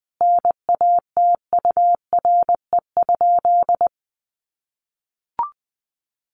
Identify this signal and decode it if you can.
Morse code, heard as a keyed tone, about 20 wpm, 700 Hz. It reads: NATURE?
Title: Sentences from Top 600 Words 20wpm